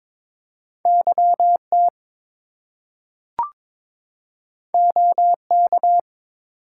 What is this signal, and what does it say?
Morse code, heard as a keyed tone, about 22 wpm, 700 Hz. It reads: YT OK
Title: US States and Canadian Provinces 22wpm